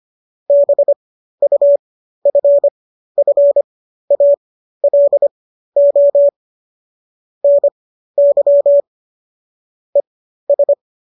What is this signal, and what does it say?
Morse code, heard as a keyed tone, about 25 wpm, spaced as for 13 wpm, 570 Hz. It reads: BUFFALO NY ES